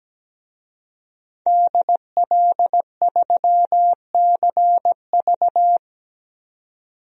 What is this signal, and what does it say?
Morse code, heard as a keyed tone, about 17 wpm, 700 Hz. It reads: DL3CV
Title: DXCC-100 Call Signs - 5 Characters - 17wpm